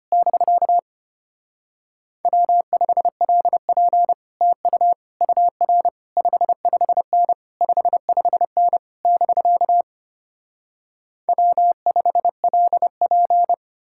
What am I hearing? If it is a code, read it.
Morse code, heard as a keyed tone, about 33 wpm, 700 Hz. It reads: <BK> W5LP TU UR 55N 55N <BK> W5LP